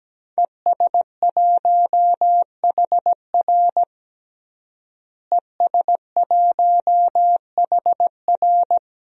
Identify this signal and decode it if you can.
Morse code, heard as a keyed tone, about 17 wpm, 700 Hz. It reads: ES1HR ES1HR
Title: DXCC-100 Call Signs - 5 Characters - Repeated 3x - 17wpm